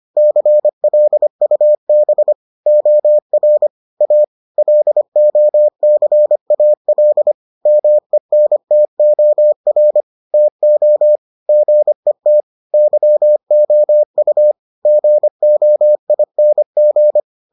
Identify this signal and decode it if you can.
Morse code, heard as a keyed tone, about 25 wpm, 600 Hz. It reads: CLUB OR A LOCAL MENTOR TO GET YOU GOING